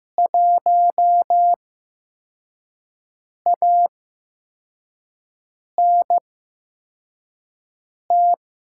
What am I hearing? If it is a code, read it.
Morse code, heard as a keyed tone, about 15 wpm, spaced as for 4 wpm, 700 Hz. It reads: 1ANT
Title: USA Call Signs - 5 Characters - 8x Character Spacing - 15wpm